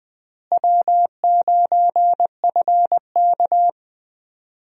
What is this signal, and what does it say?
Morse code, heard as a keyed tone, about 20 wpm, 700 Hz. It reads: W9FK
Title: USA Call Signs - 4 Characters - Repeated 2x - 20wpm